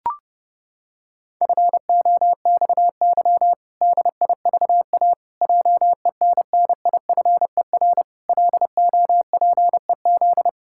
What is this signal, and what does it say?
Morse code, heard as a keyed tone, about 30 wpm, 700 Hz. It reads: FOXY DIVA JENNIFER LOPEZ
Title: Pangram Challenge v2 - 30wpm